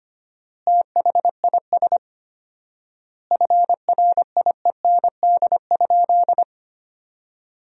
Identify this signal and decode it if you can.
Morse code, heard as a keyed tone, about 25 wpm, 700 Hz. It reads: THIS FRIEND?
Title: 4x Word-Spacing - Sentences from Top 300 Words - 25wpm